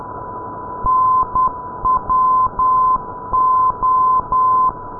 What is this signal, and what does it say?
Morse code, heard as a keyed tone, about 10 wpm, 1030 Hz. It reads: NWO
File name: pol-505 ndb oksywie nwo-494-kiwi